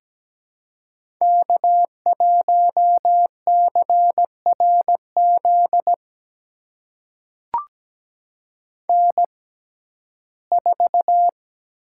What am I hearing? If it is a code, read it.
Morse code, heard as a keyed tone, about 17 wpm, spaced as for 12 wpm, 700 Hz. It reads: K1CRZ N4
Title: USA Call Signs - 5 Characters - 6x Character Spacing - 17wpm